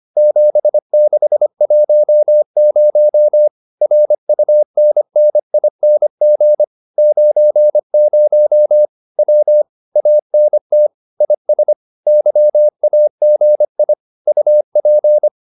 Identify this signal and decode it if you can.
Morse code, heard as a keyed tone, about 25 wpm, 600 Hz. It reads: 7610 RUNNING 90 W ANT IS YAGI UP
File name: QSO307a 25